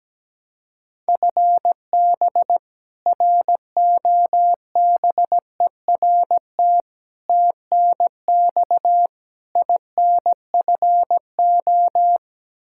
Morse code, heard as a keyed tone, about 17 wpm, 700 Hz. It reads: FB ROBERT TNX INFO
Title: K1USN SST QSO Practice 17wpm